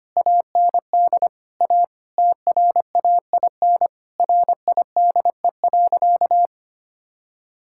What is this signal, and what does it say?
Morse code, heard as a keyed tone, about 25 wpm, 700 Hz. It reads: AND A TRAIN RIDE.
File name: practice / 25wpm / Difficult Sentences with Hints - 25wpm